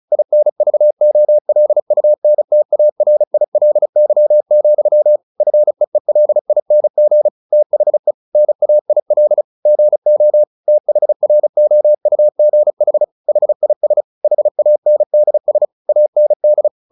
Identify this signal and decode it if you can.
Morse code, heard as a keyed tone, about 34 wpm, 600 Hz. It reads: INVOLUNTARILY, FEELING THE NAIL GO THROUGH HIS HANDS AND
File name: the old man and the sea 2 35wpm